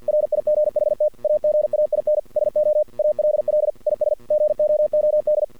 Morse code, heard as a keyed tone, about 48 wpm, 600 Hz. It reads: RIGHT AGAIN SO ALL IS GOOD
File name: archived / CRSnet